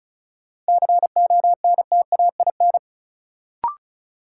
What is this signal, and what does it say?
Morse code, heard as a keyed tone, about 36 wpm, 700 Hz. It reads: CONTAIN
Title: Sentences from Top 500 Words - New Word Drill - 35wpm